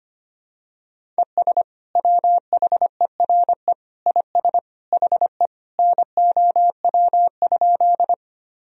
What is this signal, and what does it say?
Morse code, heard as a keyed tone, about 25 wpm, 700 Hz. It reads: ES WHERE IS HE NOW?
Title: Sentences from Top 200 Words and Abbreviations 25wpm